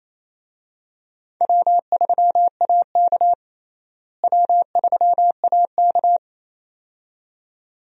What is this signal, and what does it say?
Morse code, heard as a keyed tone, about 28 wpm, 700 Hz. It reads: W3AK W3AK
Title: USA Call Signs - 4 Characters - Repeated 2x - 28wpm